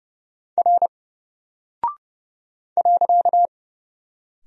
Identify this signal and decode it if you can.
Morse code, heard as a keyed tone, about 30 wpm, 700 Hz. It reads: R .
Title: Single Character (Letter:Number:Punctuation:Prosign) Alt 30wpm